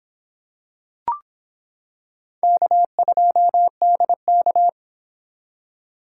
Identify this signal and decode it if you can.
Morse code, heard as a keyed tone, about 25 wpm, 700 Hz. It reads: K2DK